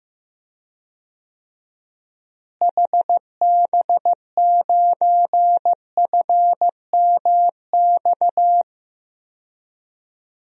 Morse code, heard as a keyed tone, about 15 wpm, 700 Hz. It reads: HB9FMX
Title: DXCC-100 Call Signs - 6 Characters - Repeated 2x - 15wpm